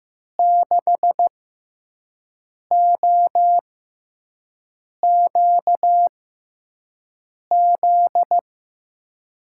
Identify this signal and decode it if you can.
Morse code, heard as a keyed tone, about 16 wpm, spaced as for 5 wpm, 700 Hz. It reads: 6OQZ
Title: USA Call Signs - 6 Characters - 6x Character Spacing - 15wpm